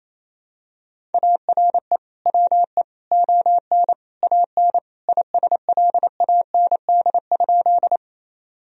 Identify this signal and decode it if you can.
Morse code, heard as a keyed tone, about 28 wpm, 700 Hz. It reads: ARE WE ON AN ISLAND?
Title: Sentences from Top 1000 Words - Review Part 3 - 28wpm